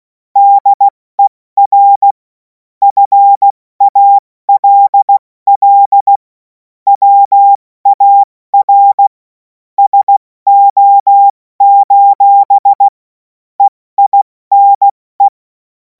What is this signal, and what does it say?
Morse code, heard as a keyed tone, about 16 wpm, 800 Hz. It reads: DER FALL WAR SO: EINE
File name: Morsekurs / Witze / Witz 11. August 2026 70 F80